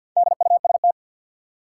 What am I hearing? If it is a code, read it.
Morse code, heard as a keyed tone, about 51 wpm, 700 Hz. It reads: DUST
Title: Four Letter Words - 50wpm